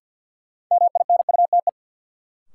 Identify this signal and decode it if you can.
Morse code, heard as a keyed tone, about 51 wpm, 700 Hz. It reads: MINUTE
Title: Sentences from Top 500 Words - New Word Drill - 50wpm